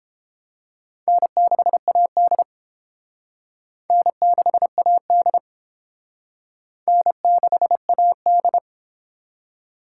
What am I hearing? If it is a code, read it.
Morse code, heard as a keyed tone, about 28 wpm, 700 Hz. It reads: N6AD N6AD N6AD